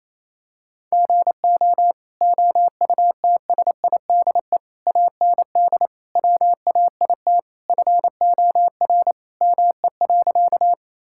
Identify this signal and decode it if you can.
Morse code, heard as a keyed tone, about 28 wpm, 700 Hz. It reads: GO OUTSIDE AND WAIT FOR ME.
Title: Sentences from Top 1000 Words - Review Part 6 - 28wpm